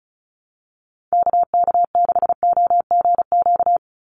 Morse code, heard as a keyed tone, about 34 wpm, 700 Hz. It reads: KK6OGQ
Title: USA Call Signs 35wpm